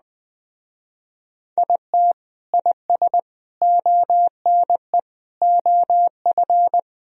Morse code, heard as a keyed tone, about 20 wpm, 700 Hz. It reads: IT IS ONE OF
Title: Speed-Racing - Sets of 5 Words 17wpm